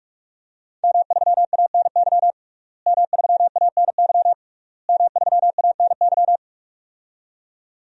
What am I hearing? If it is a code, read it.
Morse code, heard as a keyed tone, about 41 wpm, 700 Hz. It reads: M3ANY M3ANY M3ANY
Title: DXCC-100 Call Signs - 5 Characters - Repeated 3x - 45wpm